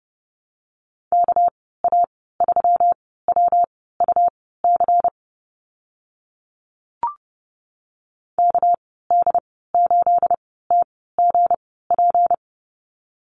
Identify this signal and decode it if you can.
Morse code, heard as a keyed tone, about 30 wpm, spaced as for 17 wpm, 700 Hz. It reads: KA3WUC KD8TGP
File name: Lesson 37g - Letter x - Call Signs 30wpm